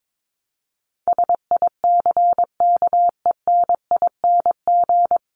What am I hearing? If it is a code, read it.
Morse code, heard as a keyed tone, about 22 wpm, 700 Hz. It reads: SICKENING
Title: A Sound Of Thunder 22wpm